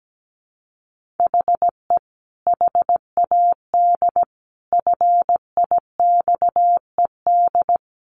Morse code, heard as a keyed tone, about 17 wpm, 700 Hz. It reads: HE HAD FIXED
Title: Sentences from Top 1500 Words 17wpm